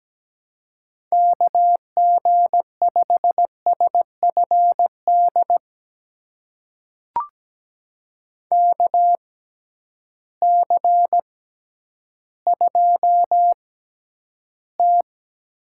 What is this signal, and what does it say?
Morse code, heard as a keyed tone, about 17 wpm, spaced as for 10 wpm, 700 Hz. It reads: KG5SFD KC2T
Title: USA Call Signs - 6 Characters - 6x Character Spacing - 17wpm